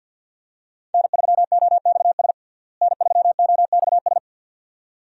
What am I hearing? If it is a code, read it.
Morse code, heard as a keyed tone, about 52 wpm, 700 Hz. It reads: N3OXS N3OXS
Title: USA Call Signs - 5 Characters - Repeated 2x - 50wpm